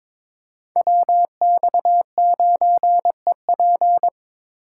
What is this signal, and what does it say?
Morse code, heard as a keyed tone, about 21 wpm, 700 Hz. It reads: WX9EP
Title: USA Call Signs - 5 Characters - 4x Character Spacing - 22wpm